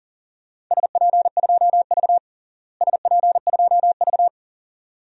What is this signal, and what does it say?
Morse code, heard as a keyed tone, about 40 wpm, 700 Hz. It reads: SP2V SP2V
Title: DXCC-100 Call Signs - 4 Characters - Repeated 2x - 40wpm